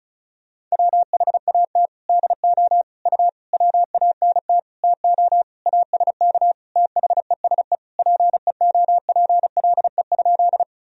Morse code, heard as a keyed tone, about 36 wpm, 700 Hz. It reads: WHAT DO U WANT TO ASK THESE PEOPLE?
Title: Sentences from Top 200 Words and Abbreviations 35wpm